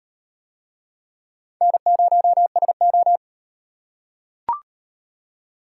Morse code, heard as a keyed tone, about 37 wpm, 700 Hz. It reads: N0SO